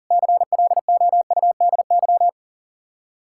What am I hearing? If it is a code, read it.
Morse code, heard as a keyed tone, about 50 wpm, 700 Hz. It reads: CLOUDY
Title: Speed-Racing Common QSO Phrases Encore - 40wpm